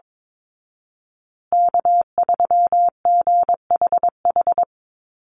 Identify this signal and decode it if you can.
Morse code, heard as a keyed tone, about 22 wpm, 700 Hz. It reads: K3GHH
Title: Call Signs Repeated 3x - 22wpm